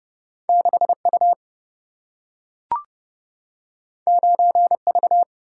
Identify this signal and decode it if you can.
Morse code, heard as a keyed tone, about 31 wpm, 700 Hz. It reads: 6 U 9 V
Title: Single-Letter Single-Number 30wpm